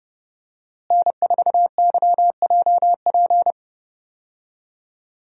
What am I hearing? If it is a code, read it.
Morse code, heard as a keyed tone, about 30 wpm, 700 Hz. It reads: N4YJP
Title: USA Call Signs - 5 Characters - Repeated 3x - 30wpm